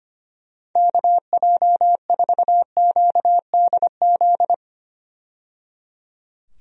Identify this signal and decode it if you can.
Morse code, heard as a keyed tone, about 25 wpm, 700 Hz. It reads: KJ4QDZ
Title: USA Call Signs - 6 Characters - 25wpm